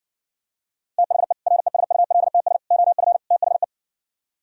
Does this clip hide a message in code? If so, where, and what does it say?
Yes: Morse code from 1.0 to 3.6 s, heard as a keyed tone, about 60 wpm, 700 Hz. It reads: THE RESULTS OF THE